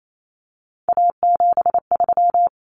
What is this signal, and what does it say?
Morse code, heard as a keyed tone, about 28 wpm, 700 Hz. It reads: A73
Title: Common Cut Numbers - 3-Digit 28wpm